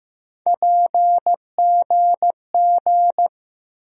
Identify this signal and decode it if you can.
Morse code, heard as a keyed tone, about 15 wpm, 700 Hz. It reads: PGG
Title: USA Call Sign Postfixes - 3 Letters 15wpm